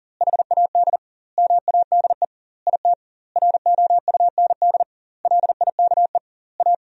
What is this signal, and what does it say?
Morse code, heard as a keyed tone, about 40 wpm, 700 Hz. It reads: HAD MADE IT ROUND LIKE A
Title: 2x Word-Spacing - Sentences from Top 500 Words - 40wpm